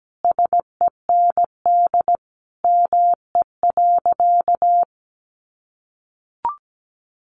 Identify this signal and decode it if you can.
Morse code, heard as a keyed tone, about 17 wpm, 700 Hz. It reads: SEND ME.
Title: Sentences from Top 1500 Words 17wpm